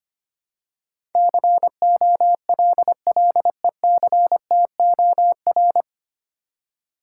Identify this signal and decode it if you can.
Morse code, heard as a keyed tone, about 25 wpm, 700 Hz. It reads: COLLECTOR
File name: Nine Letter Words - 25wpm